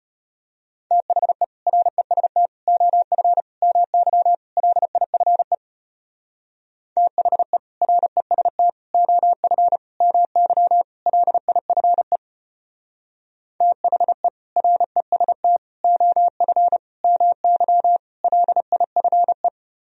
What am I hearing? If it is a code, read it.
Morse code, heard as a keyed tone, about 32 wpm, 700 Hz. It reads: THE REST OF MY LIFE THE REST OF MY LIFE THE REST OF MY LIFE